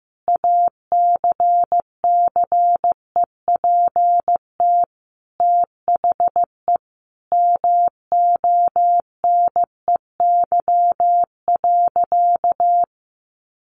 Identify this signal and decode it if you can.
Morse code, heard as a keyed tone, about 15 wpm, 700 Hz. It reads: ACCEPT THE MONEY.
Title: Sentences from Top 1300 Words 15wpm